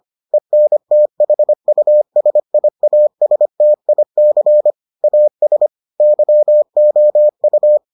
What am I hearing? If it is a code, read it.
Morse code, heard as a keyed tone, about 25 wpm, 600 Hz. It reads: ENTHUSIASTIC AS YOU